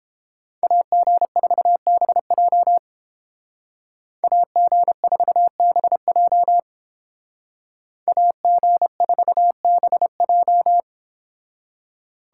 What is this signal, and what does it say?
Morse code, heard as a keyed tone, about 30 wpm, 700 Hz. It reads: AG4BJ AG4BJ AG4BJ